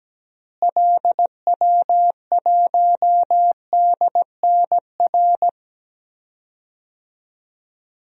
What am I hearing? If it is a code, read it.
Morse code, heard as a keyed tone, about 17 wpm, 700 Hz. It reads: LW1DNR
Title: DXCC-100 Call Signs - 6 Characters - Repeated 2x - 17wpm